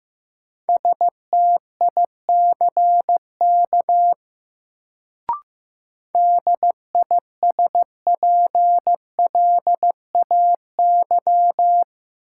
Morse code, heard as a keyed tone, about 16 wpm, 700 Hz. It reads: STICK DISPLAY
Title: Sentences from Top 1200 Words - New Word Drill Alt - 15wpm